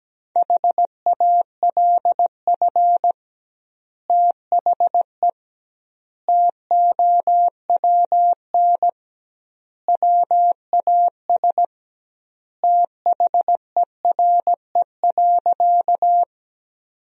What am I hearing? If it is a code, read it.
Morse code, heard as a keyed tone, about 17 wpm, 700 Hz. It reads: HALF THE TOWN WAS THERE.
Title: 2x Word-Spacing - Sentences from Top 500 Words - 17wpm